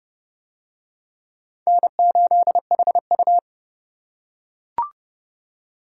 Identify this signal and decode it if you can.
Morse code, heard as a keyed tone, about 29 wpm, 700 Hz. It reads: N8HU